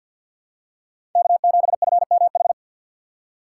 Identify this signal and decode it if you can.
Morse code, heard as a keyed tone, about 52 wpm, 690 Hz. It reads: K7FMH
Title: USA Call Signs - 2x Character Spacing - 50wpm